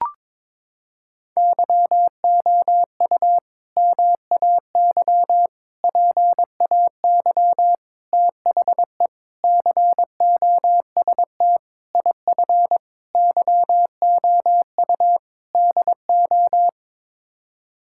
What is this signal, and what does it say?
Morse code, heard as a keyed tone, about 21 wpm, 700 Hz. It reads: YOU MAY PAY THE COST IF YOU DO
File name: practice / 22wpm / Sets of 8 Words - 22wpm